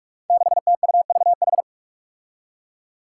Morse code, heard as a keyed tone, about 46 wpm, 700 Hz. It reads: 6TUVH
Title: Five Challenging Letter-Number Combinations 45wpm